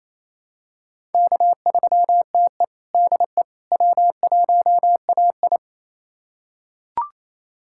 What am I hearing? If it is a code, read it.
Morse code, heard as a keyed tone, about 28 wpm, 700 Hz. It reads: K3TE DE W1AI
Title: USA Callsign DE Callsign - 4 Characters - 28wpm